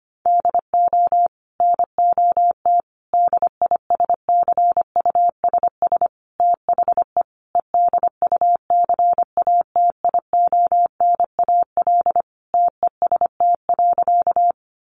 Morse code, heard as a keyed tone, about 25 wpm, 700 Hz. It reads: DO NOT DISCUSS THE EDUCATIONAL TEST.
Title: Sentences from Top 1200 Words 25wpm